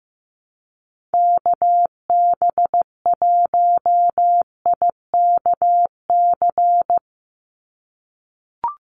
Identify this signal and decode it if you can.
Morse code, heard as a keyed tone, about 15 wpm, 700 Hz. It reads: KB1IKC
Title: USA Call Signs 15wpm